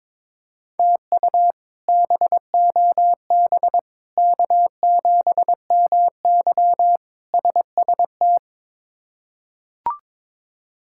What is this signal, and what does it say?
Morse code, heard as a keyed tone, about 22 wpm, 700 Hz. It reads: TU BOB K7MY SST
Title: K1USN SST QSO Practice - 22wpm